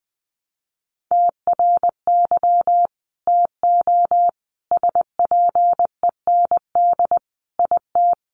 Morse code, heard as a keyed tone, about 20 wpm, 700 Hz. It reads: TRY TO SPEND IT
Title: Sentences from Top 800 Words 20wpm